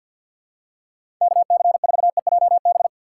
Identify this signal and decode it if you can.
Morse code, heard as a keyed tone, about 53 wpm, 700 Hz. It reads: KC4EJB
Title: USA Call Signs - 6x Character Spacing - 50wpm